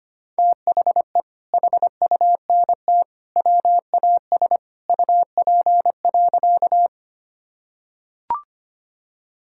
Morse code, heard as a keyed tone, about 25 wpm, 700 Hz. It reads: THE HUNT WAS UP.
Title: Sentences from Top 1600 Words 25wpm